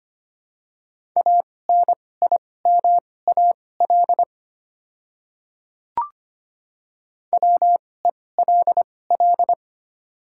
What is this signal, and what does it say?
Morse code, heard as a keyed tone, about 25 wpm, spaced as for 18 wpm, 700 Hz. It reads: ANIMAL WELL
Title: 2x Character Spacing - Top 200 Words - 25wpm